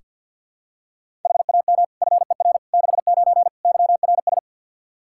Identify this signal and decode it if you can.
Morse code, heard as a keyed tone, about 49 wpm, 700 Hz. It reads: HAM FER 69 YRS
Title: K1USN SST QSO Practice 50wpm